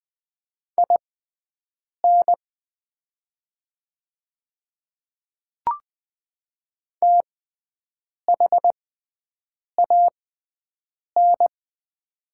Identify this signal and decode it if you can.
Morse code, heard as a keyed tone, about 20 wpm, spaced as for 7 wpm, 700 Hz. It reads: IN THAN
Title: 6x Character Spacing - Top 100 Words - 20wpm